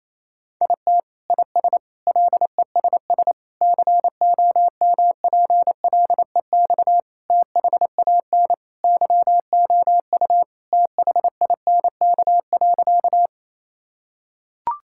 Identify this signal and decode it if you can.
Morse code, heard as a keyed tone, about 28 wpm, 700 Hz. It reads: IT IS LESS COMPLEX THAN YOU THINK.
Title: Sentences from Top 1000 Words - Review Part 6 - 28wpm